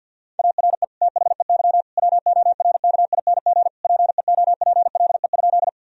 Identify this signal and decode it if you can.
Morse code, heard as a keyed tone, about 50 wpm, 700 Hz. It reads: ARE THEY WORKING PEOPLE?
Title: Sentences from Top 1000 Words - Review Part 4 - 50wpm